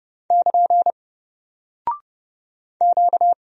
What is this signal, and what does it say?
Morse code, heard as a keyed tone, about 30 wpm, 700 Hz. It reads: <KN> Q
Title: Single Character (Letter:Number:Punctuation:Prosign) 30wpm